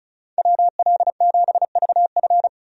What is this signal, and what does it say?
Morse code, heard as a keyed tone, about 36 wpm, 700 Hz. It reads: WL7VF
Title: USA Call Signs - 5 Characters - 35wpm